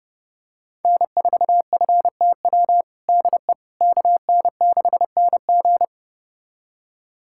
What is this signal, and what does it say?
Morse code, heard as a keyed tone, about 30 wpm, 700 Hz. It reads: N4FTW DE KN6NG
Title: USA Callsign DE Callsign - 5 Characters - 30wpm